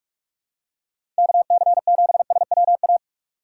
Morse code, heard as a keyed tone, about 47 wpm, 700 Hz. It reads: KC7SWA
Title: USA Call Signs - 6x Character Spacing - 45wpm